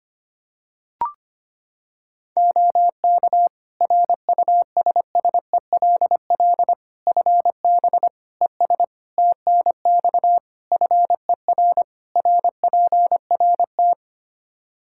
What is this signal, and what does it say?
Morse code, heard as a keyed tone, about 25 wpm, 700 Hz. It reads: OK RUSSELL FB ES TNX FER RPRT
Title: Practice QSO - Exchange 2 - Info - 25wpm